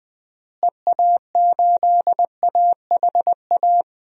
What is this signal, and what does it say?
Morse code, heard as a keyed tone, about 20 wpm, 700 Hz. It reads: EA8AHA
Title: DXCC-100 Call Signs - 6 Characters - Repeated 2x - 20wpm